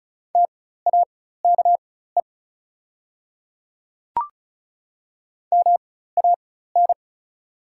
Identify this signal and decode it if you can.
Morse code, heard as a keyed tone, about 34 wpm, spaced as for 16 wpm, 690 Hz. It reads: TAKE MAN
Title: 4x Character Spacing - Top 500 Words - 35wpm